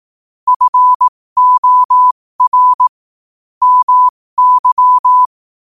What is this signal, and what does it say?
Morse code, heard as a keyed tone, about 18 wpm, spaced as for 15 wpm, 1000 Hz. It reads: FOR MY
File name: qotdfiles / 15wpm